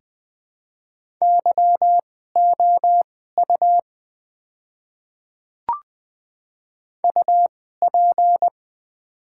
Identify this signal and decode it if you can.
Morse code, heard as a keyed tone, about 20 wpm, spaced as for 14 wpm, 700 Hz. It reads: YOU UP
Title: 2x Character Spacing - Top 300 Words - 20wpm